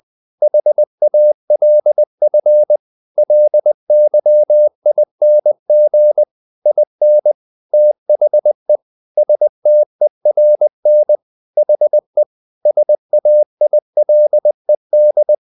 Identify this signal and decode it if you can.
Morse code, heard as a keyed tone, about 20 wpm, 600 Hz. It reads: HALF LYING IN THE STERN HE SAILED
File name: the old man and the sea part 2 20wpm